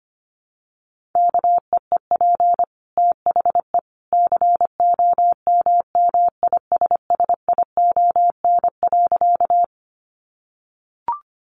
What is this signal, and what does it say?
Morse code, heard as a keyed tone, about 25 wpm, 700 Hz. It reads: KEEP THE COMMISSION.
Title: Sentences from Top 1400 Words 25wpm